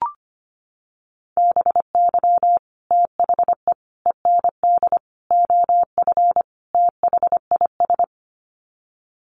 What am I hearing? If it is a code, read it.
Morse code, heard as a keyed tone, about 25 wpm, 700 Hz. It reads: BY THE END OF THIS
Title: Sets of 5 Words Encore 25wpm